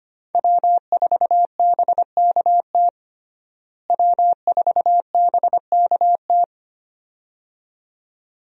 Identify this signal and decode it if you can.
Morse code, heard as a keyed tone, about 25 wpm, 700 Hz. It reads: W4BKT W4BKT
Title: USA Call Signs - 5 Characters - Repeated 2x - 25wpm